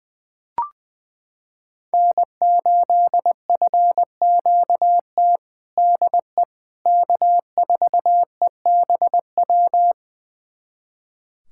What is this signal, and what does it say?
Morse code, heard as a keyed tone, about 21 wpm, 700 Hz. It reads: N8FQT DE K4EBW
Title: USA Callsign DE Callsign - 5 Characters - 20wpm